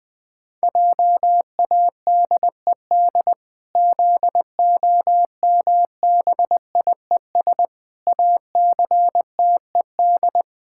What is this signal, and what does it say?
Morse code, heard as a keyed tone, about 21 wpm, 700 Hz. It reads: JADED ZOMBIES ACTED
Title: Pangram Challenge v2 - 20wpm